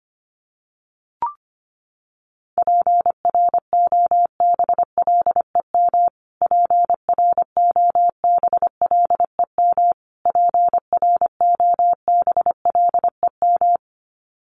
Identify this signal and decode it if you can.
Morse code, heard as a keyed tone, about 25 wpm, 700 Hz. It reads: PROBLEM PROBLEM PROBLEM
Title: Top 500 Words Repeated 25wpm